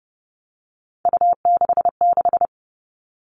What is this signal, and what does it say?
Morse code, heard as a keyed tone, about 30 wpm, 700 Hz. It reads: U66
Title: All Cut Numbers - 3-Digit 30wpm